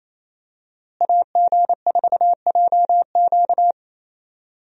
Morse code, heard as a keyed tone, about 28 wpm, 700 Hz. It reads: AG4JQ
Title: USA Call Signs - 5 Characters - Repeated 2x - 28wpm